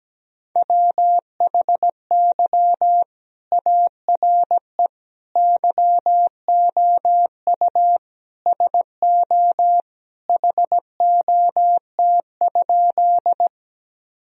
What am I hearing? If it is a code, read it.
Morse code, heard as a keyed tone, about 17 wpm, 700 Hz. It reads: WHY ARE YOU SO HOT?
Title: Sentences from Top 1000 Words - Review Part 3 - 17wpm